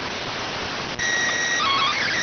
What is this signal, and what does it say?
SSTV (slow-scan television), PD 290.